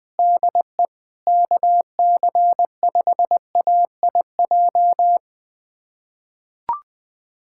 Morse code, heard as a keyed tone, about 20 wpm, 700 Hz. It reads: DE KC5AIJ
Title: USA Callsign DE Callsign - 6 Characters - 20wpm